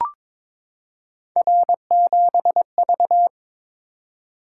Morse code, heard as a keyed tone, about 23 wpm, 700 Hz. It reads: R7V
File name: practice / 22wpm / Three Letter-Number Combinations - 22wpm